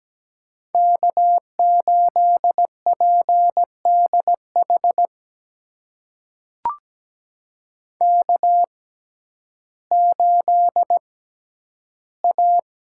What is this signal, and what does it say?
Morse code, heard as a keyed tone, about 17 wpm, spaced as for 10 wpm, 700 Hz. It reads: K8PDH K8A
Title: USA Call Signs - 5 Characters - 6x Character Spacing - 17wpm